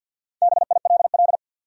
Morse code, heard as a keyed tone, about 51 wpm, 700 Hz. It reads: BILL
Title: USA Common Names - Max 4 Letters - 50wpm